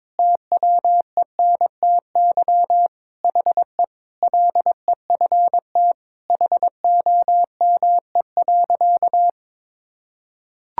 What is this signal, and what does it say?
Morse code, heard as a keyed tone, about 21 wpm, 700 Hz. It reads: TWENTY HE LEFT HOME.
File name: practice / 22wpm / Sentences with 4-Word Phrases - 22wpm